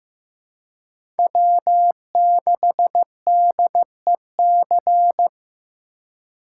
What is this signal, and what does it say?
Morse code, heard as a keyed tone, about 15 wpm, 700 Hz. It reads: W6DEC
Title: USA Call Signs - 5 Characters - 4x Character Spacing - 15wpm